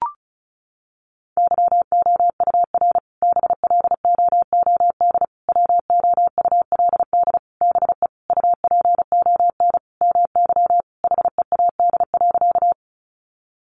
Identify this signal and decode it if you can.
Morse code, heard as a keyed tone, about 34 wpm, 700 Hz. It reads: YOUR BLOOD WOULD BE UPON MY HEAD.
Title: Sentences from Top 1100 Words 35wpm